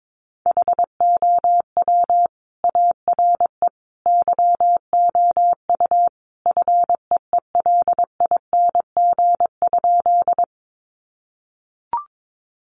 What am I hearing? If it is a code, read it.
Morse code, heard as a keyed tone, about 22 wpm, 700 Hz. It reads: HOW ARE YOU FEELING?
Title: Sentences from Top 1200 Words 22wpm